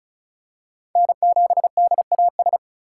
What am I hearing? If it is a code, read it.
Morse code, heard as a keyed tone, about 34 wpm, 700 Hz. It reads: N7DAS
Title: USA Call Signs - 3x Character Spacing - 35wpm